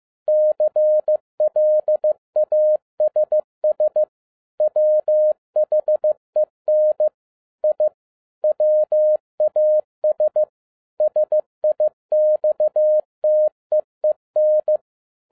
Morse code, heard as a keyed tone, about 15 wpm, 610 Hz. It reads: CLASS WHEN I WAS SIXTEEN